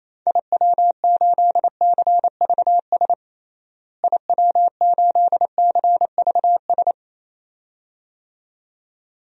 Morse code, heard as a keyed tone, about 28 wpm, 700 Hz. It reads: IW8CVS IW8CVS
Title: DXCC-100 Call Signs - 6 Characters - Repeated 2x - 28wpm